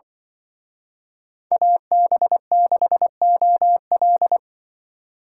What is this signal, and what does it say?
Morse code, heard as a keyed tone, about 25 wpm, 700 Hz. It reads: AB6OL
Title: Speed Racing - USA 2x2 Call Signs 20wpm